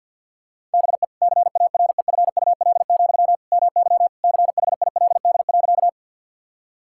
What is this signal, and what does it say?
Morse code, heard as a keyed tone, about 50 wpm, 700 Hz. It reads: BE CAREFUL, MY CHILD.
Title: Sentences from Top 1600 Words 50wpm